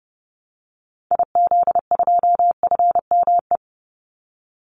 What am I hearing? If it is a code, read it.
Morse code, heard as a keyed tone, about 30 wpm, 700 Hz. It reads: IZ2FME
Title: Call Signs - 30wpm